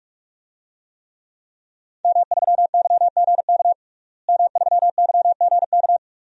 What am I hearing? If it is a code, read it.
Morse code, heard as a keyed tone, about 46 wpm, 700 Hz. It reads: M3YGK M3YGK
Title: DXCC-100 Call Signs - 5 Characters - Repeated 2x - 45wpm